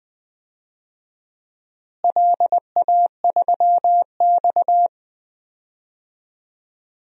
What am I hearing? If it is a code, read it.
Morse code, heard as a keyed tone, about 20 wpm, 700 Hz. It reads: LA3X
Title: DXCC-100 Call Signs - 4 Characters - Repeated 2x - 20wpm